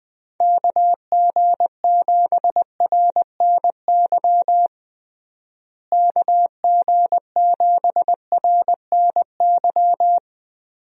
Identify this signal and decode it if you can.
Morse code, heard as a keyed tone, about 21 wpm, 700 Hz. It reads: KG7RNY KG7RNY
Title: USA Call Signs - 6 Characters - Repeated 3x - 20wpm